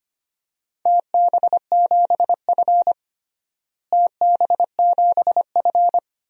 Morse code, heard as a keyed tone, about 25 wpm, 700 Hz. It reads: TB7F TB7F
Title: DXCC-100 Call Signs - 4 Characters - Repeated 2x - 25wpm